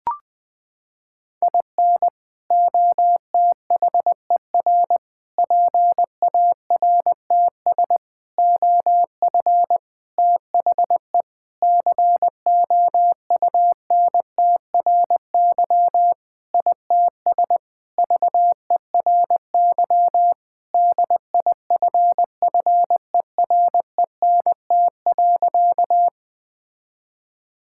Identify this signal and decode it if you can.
Morse code, heard as a keyed tone, about 20 wpm, 700 Hz. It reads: IN OTHER PARTS OF THE COUNTRY ITS VERY DIFFERENT.
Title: Sentences with 5-Word Phrases - 20wpm